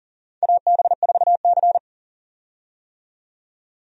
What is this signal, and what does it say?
Morse code, heard as a keyed tone, about 40 wpm, 700 Hz. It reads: AB4C
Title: USA Call Signs - 4 Characters - 40wpm